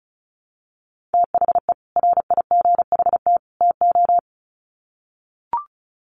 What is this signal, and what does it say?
Morse code, heard as a keyed tone, about 34 wpm, 700 Hz. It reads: THE RIGHT TO
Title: Sets of 3 Words 35wpm